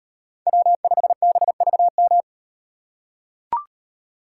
Morse code, heard as a keyed tone, about 38 wpm, 700 Hz. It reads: W5BVM